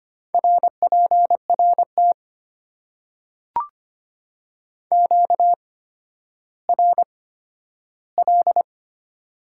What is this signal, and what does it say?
Morse code, heard as a keyed tone, about 25 wpm, spaced as for 12 wpm, 700 Hz. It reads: RPRT QRL
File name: Top 90 QSO Elements - 8x Character Spacing - 25wpm